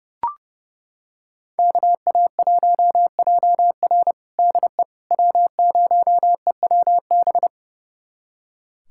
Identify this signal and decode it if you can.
Morse code, heard as a keyed tone, about 30 wpm, 700 Hz. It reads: KA1JR DE W0EWB
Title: USA Callsign DE Callsign - 5 Characters - 30wpm